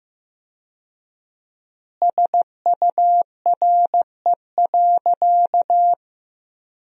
Morse code, heard as a keyed tone, about 16 wpm, 700 Hz. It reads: SURE.
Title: 4x Word-Spacing - Sentences from Top 500 Words - 15wpm